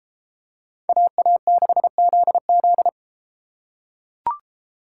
Morse code, heard as a keyed tone, about 33 wpm, 700 Hz. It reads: AA6ZZ